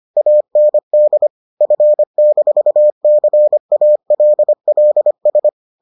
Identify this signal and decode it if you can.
Morse code, heard as a keyed tone, about 25 wpm, 600 Hz. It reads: AND F-CALLS